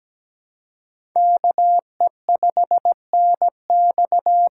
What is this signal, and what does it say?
Morse code, heard as a keyed tone, about 17 wpm, 700 Hz. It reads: KE5NX
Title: USA Call Signs - 6 Characters - 6x Character Spacing - 17wpm